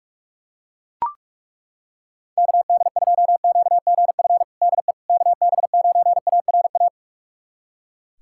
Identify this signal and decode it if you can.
Morse code, heard as a keyed tone, about 42 wpm, 700 Hz. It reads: KD2QGF DE KB9ARA
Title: USA Callsign DE Callsign - 6 Characters - 45wpm